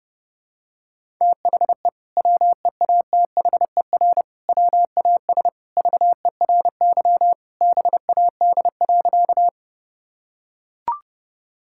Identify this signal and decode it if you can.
Morse code, heard as a keyed tone, about 30 wpm, 700 Hz. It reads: THE WEATHER WAS VERY BAD.
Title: Sentences from Top 1000 Words - Review Part 5 - 30wpm